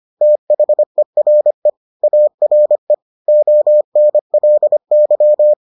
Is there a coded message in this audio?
Yes: Morse code from 0.2 to 5.5 s, heard as a keyed tone, about 25 wpm, 600 Hz. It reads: THERE ARE ONLY